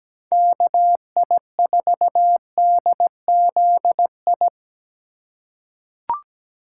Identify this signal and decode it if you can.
Morse code, heard as a keyed tone, about 17 wpm, 700 Hz. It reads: KI4DZI
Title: USA Call Signs - 6 Characters - 3x Character Spacing - 17wpm